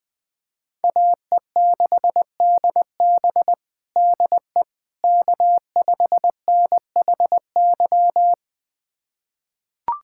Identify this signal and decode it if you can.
Morse code, heard as a keyed tone, about 20 wpm, 700 Hz. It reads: AE6DB DE K5NHY
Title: USA Callsign DE Callsign - 5 Characters - 20wpm